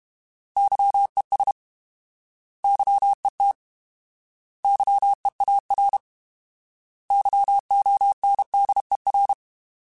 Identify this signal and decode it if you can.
Morse code, heard as a keyed tone, about 32 wpm, 800 Hz. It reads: YES YET YEAR YONDER
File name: Alphabetical / Alphabetical mp3s / Ys @30